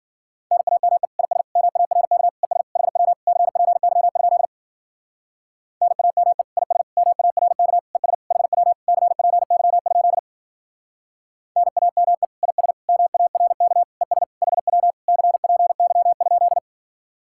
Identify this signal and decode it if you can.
Morse code, heard as a keyed tone, about 49 wpm, 700 Hz. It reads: NAME IS MARK ES HW CPY? NAME IS MARK ES HW CPY? NAME IS MARK ES HW CPY?
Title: Speed-Racing Common QSO Phrases - 40wpm